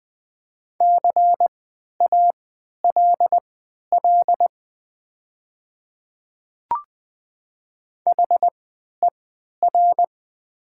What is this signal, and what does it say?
Morse code, heard as a keyed tone, about 20 wpm, spaced as for 11 wpm, 700 Hz. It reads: CALL HER
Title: 3x Character Spacing - Top 200 Words - 20wpm